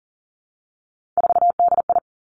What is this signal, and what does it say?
Morse code, heard as a keyed tone, about 38 wpm, 690 Hz. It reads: 4DI
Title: Three Challenging Letter-Number Combinations 40wpm